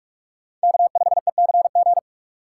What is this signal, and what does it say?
Morse code, heard as a keyed tone, about 46 wpm, 700 Hz. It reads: K5ECG
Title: USA Call Signs - 5 Characters - 4x Character Spacing - 45wpm